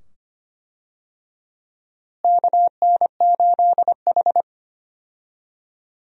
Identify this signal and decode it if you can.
Morse code, heard as a keyed tone, about 25 wpm, 700 Hz. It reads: KN8H
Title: USA Call Signs - 4 Characters - Repeated 2x - 25wpm